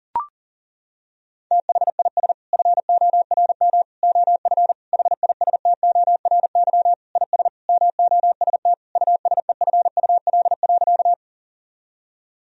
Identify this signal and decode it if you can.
Morse code, heard as a keyed tone, about 37 wpm, 700 Hz. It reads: THIS FORM OF HISTORY IS MOST USEFUL.
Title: Sentences from Top 1000 Words - Review Part 4 - 40wpm